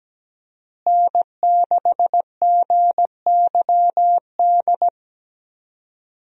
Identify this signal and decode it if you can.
Morse code, heard as a keyed tone, about 17 wpm, 700 Hz. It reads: N6GYD
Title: USA Call Signs - 5 Characters - Repeated 2x - 17wpm